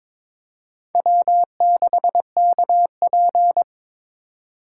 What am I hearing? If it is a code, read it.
Morse code, heard as a keyed tone, about 22 wpm, 700 Hz. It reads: W6KP
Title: USA Call Signs - 4 Characters - 8x Character Spacing - 22wpm